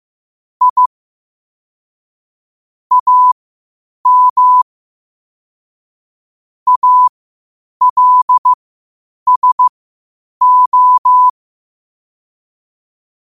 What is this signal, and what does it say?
Morse code, heard as a keyed tone, about 15 wpm, spaced as for 8 wpm, 1000 Hz. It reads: I AM ALSO
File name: qotdfiles / 8wpm